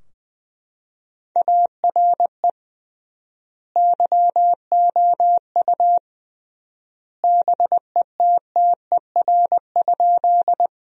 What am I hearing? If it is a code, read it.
Morse code, heard as a keyed tone, about 20 wpm, 700 Hz. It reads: ARE YOU BETTER?
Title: 3x Word-Spacing - Sentences from Top 400 Words - 20wpm